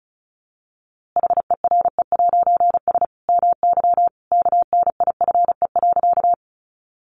Morse code, heard as a keyed tone, about 34 wpm, 700 Hz. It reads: HERE'S MY KNIFE.
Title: A Sound Of Thunder 35wpm